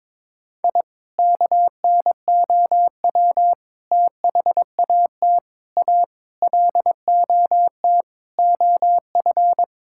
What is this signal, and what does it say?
Morse code, heard as a keyed tone, about 22 wpm, 700 Hz. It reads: I KNOW THAT A LOT OF
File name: practice / 22wpm / Sentences with 4-Word Phrases - 22wpm